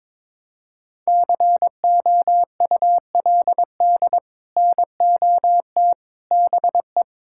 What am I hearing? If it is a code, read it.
Morse code, heard as a keyed tone, about 22 wpm, 700 Hz. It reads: COULD NOT BE
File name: practice / 22wpm / Sets of 6 Words - 22wpm